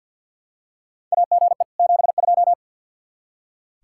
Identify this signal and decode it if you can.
Morse code, heard as a keyed tone, about 46 wpm, 700 Hz. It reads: AGE 72
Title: K1USN SST QSO Practice 50wpm